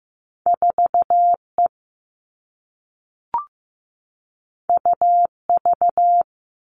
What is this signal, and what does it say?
Morse code, heard as a keyed tone, about 15 wpm, 700 Hz. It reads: 4E UV
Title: All Cut Numbers - 2-Digit 15wpm